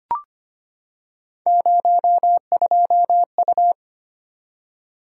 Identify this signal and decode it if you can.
Morse code, heard as a keyed tone, about 25 wpm, 700 Hz. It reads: 02U
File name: Three Letter-Number Combinations - 25wpm